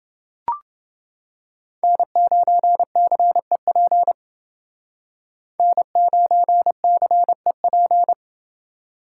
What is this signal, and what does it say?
Morse code, heard as a keyed tone, about 29 wpm, 700 Hz. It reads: N9CEP N9CEP